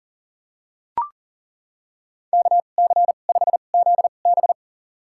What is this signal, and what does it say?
Morse code, heard as a keyed tone, about 40 wpm, spaced as for 29 wpm, 700 Hz. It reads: KC5ZB
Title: USA Call Signs - 5 Characters - 2x Character Spacing - 40wpm